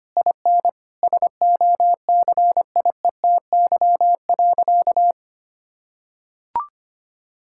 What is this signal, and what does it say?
Morse code, heard as a keyed tone, about 25 wpm, 700 Hz. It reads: IN SOCIETY.
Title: Sentences from Top 1000 Words - Review Part 6 - 25wpm